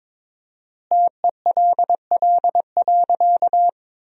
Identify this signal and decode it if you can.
Morse code, heard as a keyed tone, about 22 wpm, 700 Hz. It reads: TELL.
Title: 4x Word-Spacing - Sentences from Top 300 Words - 22wpm